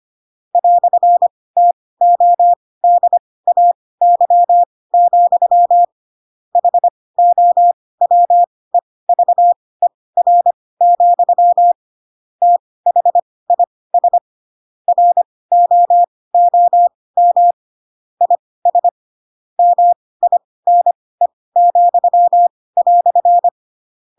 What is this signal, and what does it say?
Morse code, heard as a keyed tone, about 25 wpm, spaced as for 18 wpm, 700 Hz. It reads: "TODAY, HOWEVER, THIS ROOM IS MINE,"
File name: Pinocchio / Pinocchio ch4 12 25 ik1hge